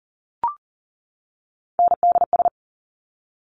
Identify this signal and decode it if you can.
Morse code, heard as a keyed tone, about 40 wpm, 700 Hz. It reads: NDS
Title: Top Three Letter Combinations in Words 40wpm